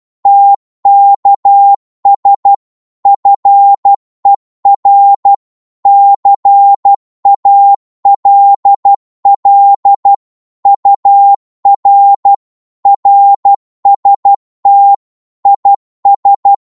Morse code, heard as a keyed tone, about 12 wpm, 800 Hz. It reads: TKS FER CALL UR RST IS
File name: Morse / qso-12wpm